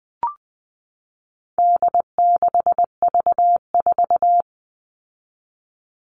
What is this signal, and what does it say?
Morse code, heard as a keyed tone, about 20 wpm, 700 Hz. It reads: D6V4
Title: All Cut Numbers - 4-Digit 20wpm